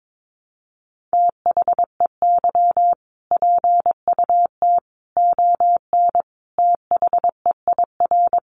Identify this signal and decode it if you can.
Morse code, heard as a keyed tone, about 22 wpm, 700 Hz. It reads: THEY PUT ON THEIR
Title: A Sound Of Thunder 22wpm